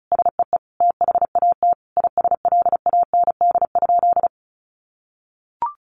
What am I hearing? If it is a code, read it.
Morse code, heard as a keyed tone, about 34 wpm, 700 Hz. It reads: SEE THAT ISLAND?
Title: Sentences from Top 500 Words 35wpm